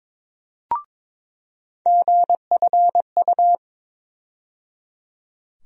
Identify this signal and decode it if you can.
Morse code, heard as a keyed tone, about 21 wpm, 700 Hz. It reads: GFU
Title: USA Call Sign Postfixes - 3 Letters 22wpm